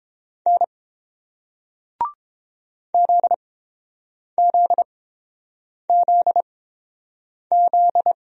Morse code, heard as a keyed tone, about 30 wpm, spaced as for 7 wpm, 700 Hz. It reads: N ZZZZ